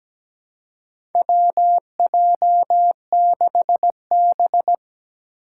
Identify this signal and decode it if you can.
Morse code, heard as a keyed tone, about 17 wpm, 700 Hz. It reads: WJ6B
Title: Four Challenging Letter-Number Combinations 17wpm